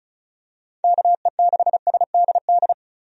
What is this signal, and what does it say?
Morse code, heard as a keyed tone, about 36 wpm, 700 Hz. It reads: KE6SDD
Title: USA Call Signs - 6 Characters - 35wpm